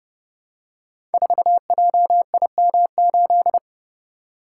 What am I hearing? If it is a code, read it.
Morse code, heard as a keyed tone, about 30 wpm, 700 Hz. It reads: 4JIM8
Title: Code Groups - 5 Letter:Numbers - 30wpm